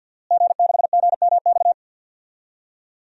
Morse code, heard as a keyed tone, about 51 wpm, 700 Hz. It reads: G6GMX
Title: DXCC-100 Call Signs - 5 Characters - Repeated 3x - 50wpm